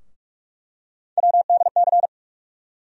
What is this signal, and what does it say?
Morse code, heard as a keyed tone, about 42 wpm, 700 Hz. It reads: WDC
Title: USA Call Sign Postfixes - 3 Letters 45wpm